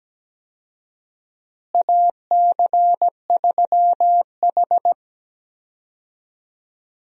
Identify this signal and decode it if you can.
Morse code, heard as a keyed tone, about 17 wpm, 700 Hz. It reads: AC3H
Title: USA Call Signs - 4 Characters - Repeated 2x - 17wpm